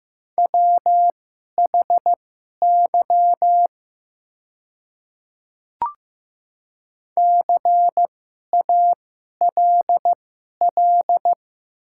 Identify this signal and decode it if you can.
Morse code, heard as a keyed tone, about 15 wpm, spaced as for 11 wpm, 700 Hz. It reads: WHY CALL
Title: 2x Character Spacing - Top 200 Words - 15wpm